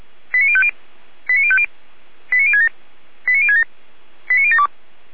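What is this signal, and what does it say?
Five-tone selective call (ZVEI), heard as quick runs of five tones, tones of about 70 ms. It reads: 89050 89050 89067 89067 89072
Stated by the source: SDR-received ZVEI five-tone selective call, code 89705.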